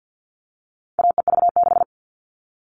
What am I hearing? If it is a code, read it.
Morse code, heard as a keyed tone, about 50 wpm, 700 Hz. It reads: AEV6
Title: All Cut Numbers - 4-Digit 50wpm